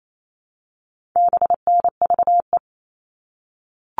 Morse code, heard as a keyed tone, about 28 wpm, 700 Hz. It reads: BNVE
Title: All Cut Numbers - 4-Digit 28wpm